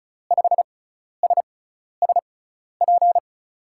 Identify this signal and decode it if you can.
Morse code, heard as a keyed tone, about 36 wpm, spaced as for 12 wpm, 700 Hz. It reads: 5SSP
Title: USA Call Signs - 6 Characters - 6x Character Spacing - 35wpm